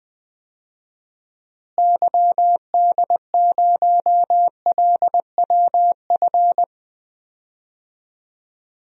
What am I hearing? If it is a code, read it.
Morse code, heard as a keyed tone, about 20 wpm, 700 Hz. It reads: YD0LWF
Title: DXCC-100 Call Signs - 6 Characters - Repeated 3x - 20wpm